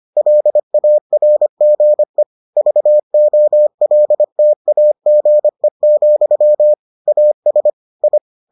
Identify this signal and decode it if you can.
Morse code, heard as a keyed tone, about 25 wpm, 600 Hz. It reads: LARGE VOLTAGE, AS I